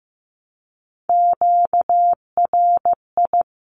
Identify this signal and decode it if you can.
Morse code, heard as a keyed tone, about 15 wpm, 700 Hz. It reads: QRI
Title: Q-Codes 15wpm